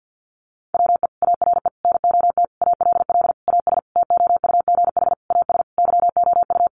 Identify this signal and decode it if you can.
Morse code, heard as a keyed tone, about 50 wpm, 700 Hz. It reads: WE ARE NOT ALL AS TOUGH AS YOU
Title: Sentences from Top 1200 Words 50wpm